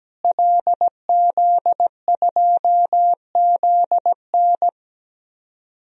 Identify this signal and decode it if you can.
Morse code, heard as a keyed tone, about 17 wpm, 700 Hz. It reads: LZ2ZN
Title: DXCC-100 Call Signs - 5 Characters - Repeated 3x - 17wpm